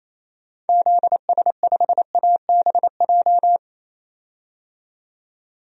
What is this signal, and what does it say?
Morse code, heard as a keyed tone, about 28 wpm, 700 Hz. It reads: ZS5ABJ
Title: DXCC-100 Call Signs - 6 Characters - 28wpm